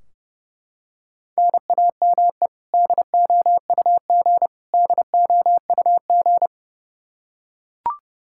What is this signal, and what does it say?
Morse code, heard as a keyed tone, about 32 wpm, 700 Hz. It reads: NAME DOUG DOUG
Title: Practice QSO - Exchange 1 - Beginning - 30wpm